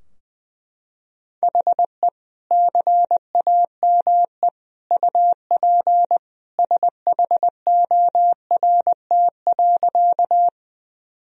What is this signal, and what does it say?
Morse code, heard as a keyed tone, about 20 wpm, 700 Hz. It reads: HE CAME UP SHORT.
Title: Sentences from Top 1000 Words - Review Part 2 - 20wpm